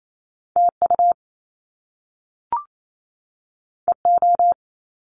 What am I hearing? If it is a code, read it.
Morse code, heard as a keyed tone, about 28 wpm, 700 Hz. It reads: TU EO
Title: Top Two Letter Combinations in Words 28wpm